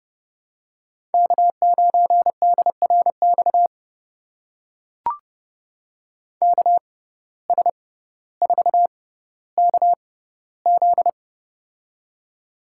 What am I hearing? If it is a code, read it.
Morse code, heard as a keyed tone, about 32 wpm, spaced as for 16 wpm, 700 Hz. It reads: K9DRX KS4KZ
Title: USA Call Signs - 6x Character Spacing - 30wpm